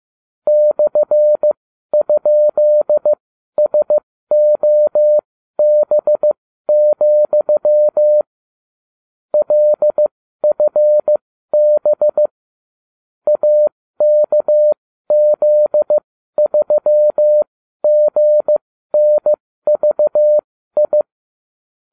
Morse code, heard as a keyed tone, about 15 wpm, spaced as for 12 wpm, 600 Hz. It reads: /?SOB, LFB AKZ3GNVI